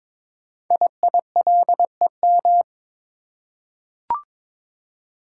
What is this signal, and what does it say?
Morse code, heard as a keyed tone, about 22 wpm, 700 Hz. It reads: IILEM
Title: Code Groups - 5 Letter:Numbers - 22wpm